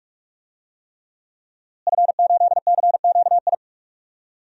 Morse code, heard as a keyed tone, about 46 wpm, 700 Hz. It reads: F8CQI
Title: DXCC-100 Call Signs - 5 Characters - Repeated 3x - 45wpm